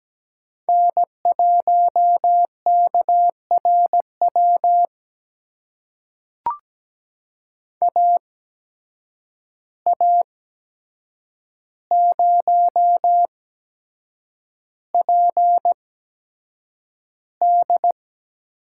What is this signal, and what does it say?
Morse code, heard as a keyed tone, about 17 wpm, spaced as for 7 wpm, 700 Hz. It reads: N1KRW AA0PD
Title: USA Call Signs - 5 Characters - 8x Character Spacing - 17wpm